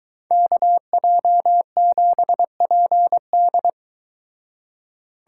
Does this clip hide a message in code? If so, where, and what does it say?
Yes: Morse code from 0.3 to 3.7 s, heard as a keyed tone, about 23 wpm, 700 Hz. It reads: KJ7PD